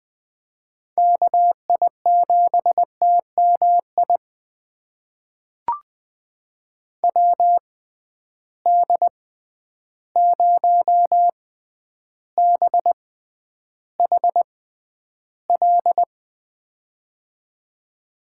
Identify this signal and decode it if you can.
Morse code, heard as a keyed tone, about 21 wpm, spaced as for 10 wpm, 700 Hz. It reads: KI7TMI WD0BHL
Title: USA Call Signs - 6 Characters - 6x Character Spacing - 20wpm